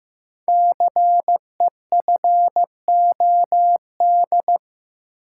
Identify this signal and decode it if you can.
Morse code, heard as a keyed tone, about 15 wpm, 700 Hz. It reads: CEFOD
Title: Code Groups - 5 Letters - 15wpm